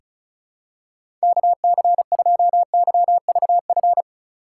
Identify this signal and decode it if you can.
Morse code, heard as a keyed tone, about 34 wpm, 700 Hz. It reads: KC2YVF
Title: USA Call Signs - 6 Characters - 3x Character Spacing - 35wpm